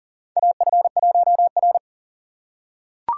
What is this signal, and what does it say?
Morse code, heard as a keyed tone, about 40 wpm, 700 Hz. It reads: AF1R
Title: USA Call Signs - 4 Characters - 3x Character Spacing - 40wpm